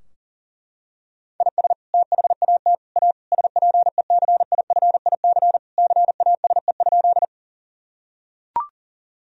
Morse code, heard as a keyed tone, about 40 wpm, 700 Hz. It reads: IS THAT A SPECIFIC CASE?
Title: Sentences from Top 1000 Words - Review Part 3 - 40wpm